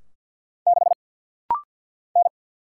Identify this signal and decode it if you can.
Morse code, heard as a keyed tone, about 50 wpm, 700 Hz. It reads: 6 N
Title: Single Letter-Number - Rapid-Fire 50wpm